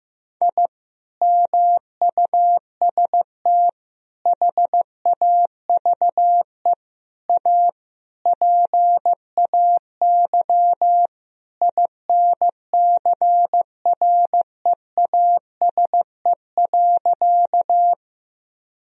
Morse code, heard as a keyed tone, about 14 wpm, 700 Hz. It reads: I MUST HAVE A PAY INCREASE.
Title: Sentences from Top 1000 Words - Review Part 4 - 15wpm